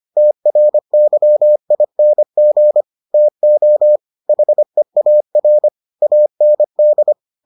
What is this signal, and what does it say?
Morse code, heard as a keyed tone, about 25 wpm, 600 Hz. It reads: TRYING TO HEAR AND